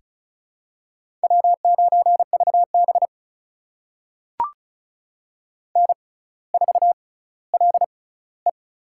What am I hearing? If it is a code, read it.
Morse code, heard as a keyed tone, about 36 wpm, spaced as for 18 wpm, 700 Hz. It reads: W9VB N4LE
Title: USA Call Signs - 4 Characters - 6x Character Spacing - 35wpm